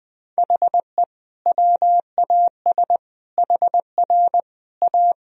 Morse code, heard as a keyed tone, about 21 wpm, 700 Hz. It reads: HE WAS HR A
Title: Sentences from Top 500 Words and Abbreviations 20wpm